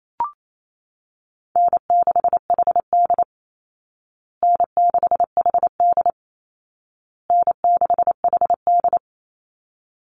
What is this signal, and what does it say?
Morse code, heard as a keyed tone, about 28 wpm, 700 Hz. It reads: N6HD N6HD N6HD
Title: Easy Call Signs Repeated 3x - 28wpm